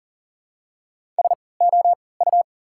Morse code, heard as a keyed tone, about 40 wpm, spaced as for 23 wpm, 700 Hz. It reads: SOU
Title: 3x Character Spacing - Top 200 Words - 40wpm